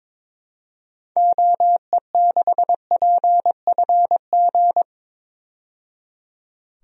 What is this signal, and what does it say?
Morse code, heard as a keyed tone, about 22 wpm, 700 Hz. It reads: OE6PFG
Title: DXCC-100 Call Signs - 6 Characters - 22wpm